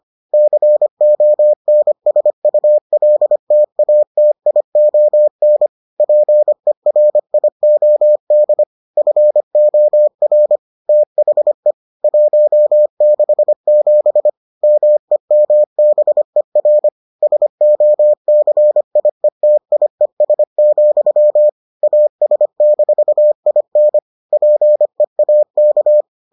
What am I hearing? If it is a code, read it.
Morse code, heard as a keyed tone, about 25 wpm, 600 Hz. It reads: CONSULTATION PERIOD FOR THE 167 MEMBER SOCIETIES, AS-IN PEAK